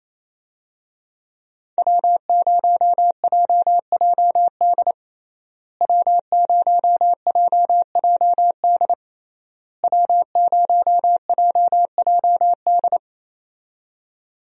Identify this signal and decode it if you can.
Morse code, heard as a keyed tone, about 28 wpm, 700 Hz. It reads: W0JJD W0JJD W0JJD
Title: USA Call Signs - 5 Characters - Repeated 3x - 28wpm